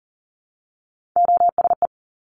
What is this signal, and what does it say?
Morse code, heard as a keyed tone, about 38 wpm, 700 Hz. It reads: OSE
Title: Top Three Letter Combinations in Words 40wpm